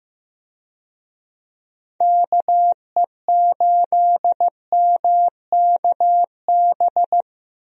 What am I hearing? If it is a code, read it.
Morse code, heard as a keyed tone, about 15 wpm, 700 Hz. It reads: KE8MKB
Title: USA Call Signs - 6 Characters - Repeated 2x - 15wpm